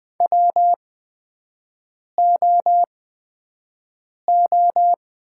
Morse code, heard as a keyed tone, about 20 wpm, spaced as for 5 wpm, 700 Hz. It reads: WOO
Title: 8x Character Spacing - Top 300 Words - 20wpm